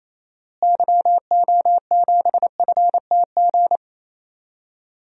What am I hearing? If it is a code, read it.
Morse code, heard as a keyed tone, about 27 wpm, 700 Hz. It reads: YO7FTG
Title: DXCC-100 Call Signs - 6 Characters - Repeated 3x - 28wpm